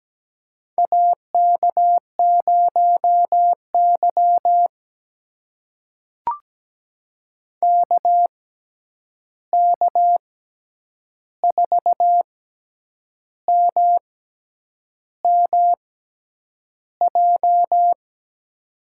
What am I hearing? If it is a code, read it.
Morse code, heard as a keyed tone, about 17 wpm, spaced as for 8 wpm, 700 Hz. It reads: AK0Y KK4MMJ
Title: USA Call Signs - 6x Character Spacing - 17wpm